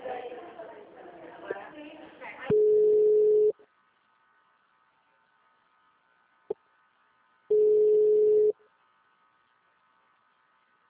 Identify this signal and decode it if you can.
Ringback tone (Europe, 425 Hz), with speech.
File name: out- -2008- - - .6188